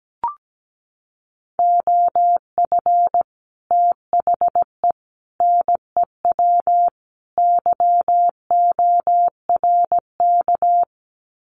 Morse code, heard as a keyed tone, about 17 wpm, 700 Hz. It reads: OF THE NEW YORK
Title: Sets of 4 Words 17wpm